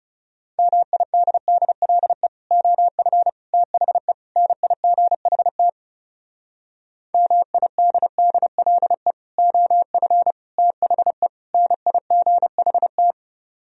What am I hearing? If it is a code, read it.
Morse code, heard as a keyed tone, about 38 wpm, 700 Hz. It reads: MIDDLE OF THE NIGHT MIDDLE OF THE NIGHT